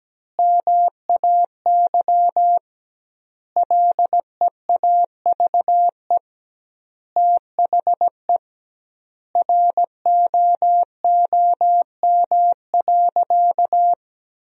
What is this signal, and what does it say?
Morse code, heard as a keyed tone, about 17 wpm, 700 Hz. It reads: MAY LEAVE THE ROOM.
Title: 2x Word-Spacing - Sentences from Top 400 Words - 17wpm